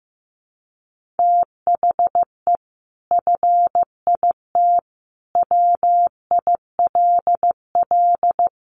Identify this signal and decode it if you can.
Morse code, heard as a keyed tone, about 15 wpm, 700 Hz. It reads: THE FIT WILL
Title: Sentences from Top 1500 Words 15wpm